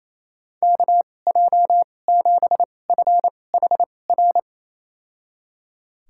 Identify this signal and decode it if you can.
Morse code, heard as a keyed tone, about 28 wpm, spaced as for 20 wpm, 700 Hz. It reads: KJ7FHR
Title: USA Call Signs - 6 Characters - 2x Character Spacing - 28wpm